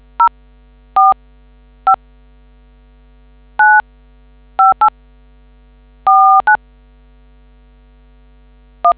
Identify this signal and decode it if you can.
DTMF, heard as telephone touch tones, tones of about 110 ms. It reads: 045 95849 1